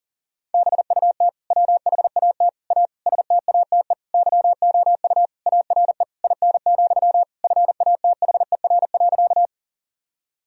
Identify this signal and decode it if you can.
Morse code, heard as a keyed tone, about 40 wpm, 700 Hz. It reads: BUT WHAT A STATE YOU ARE IN, FATHER.
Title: Sentences from Top 1000 Words - Review Part 2 - 40wpm